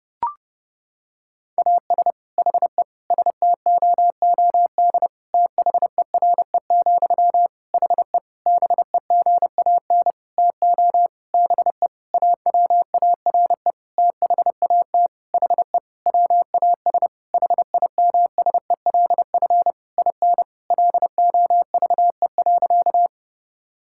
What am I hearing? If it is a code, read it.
Morse code, heard as a keyed tone, about 30 wpm, 700 Hz. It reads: AS HE STOOD THERE, HE BEGAN TO BE AWARE THAT HE WAS HIMSELF IN LOVE.
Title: Sentences from Top 1600 Words 30wpm